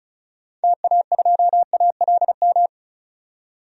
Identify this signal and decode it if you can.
Morse code, heard as a keyed tone, about 34 wpm, 700 Hz. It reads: TA2ALM
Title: DXCC-100 Call Signs - 6 Characters - 35wpm